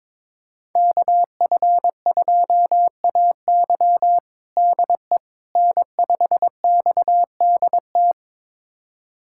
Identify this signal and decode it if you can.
Morse code, heard as a keyed tone, about 22 wpm, 700 Hz. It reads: KF2AY DE N5XDT
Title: USA Callsign DE Callsign - 5 Characters - 22wpm